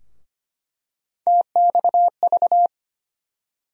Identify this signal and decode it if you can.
Morse code, heard as a keyed tone, about 25 wpm, 700 Hz. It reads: TXV
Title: USA Call Sign Postfixes - 3 Letters 25wpm